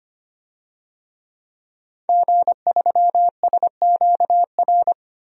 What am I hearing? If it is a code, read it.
Morse code, heard as a keyed tone, about 25 wpm, 700 Hz. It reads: G3SQR
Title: DXCC-100 Call Signs - 5 Characters - Repeated 2x - 25wpm